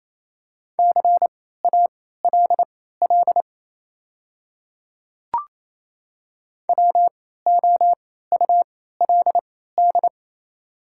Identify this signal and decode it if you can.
Morse code, heard as a keyed tone, about 28 wpm, spaced as for 16 wpm, 690 Hz. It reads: CALL WOULD
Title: 3x Character Spacing - Top 100 Words - 28wpm